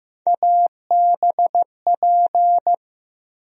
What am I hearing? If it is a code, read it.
Morse code, heard as a keyed tone, about 15 wpm, 700 Hz. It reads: ABP
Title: USA Call Sign Postfixes - 3 Letters 15wpm